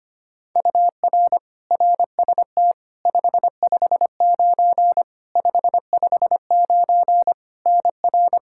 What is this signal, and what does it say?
Morse code, heard as a keyed tone, about 25 wpm, 700 Hz. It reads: UR RST 559 559 NR
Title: Practice QSO - Exchange 1 - Beginning - 25wpm